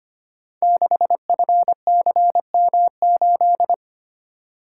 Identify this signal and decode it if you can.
Morse code, heard as a keyed tone, about 25 wpm, 700 Hz. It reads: 6FCM8
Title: Code Groups - 5 Letter:Numbers - 25wpm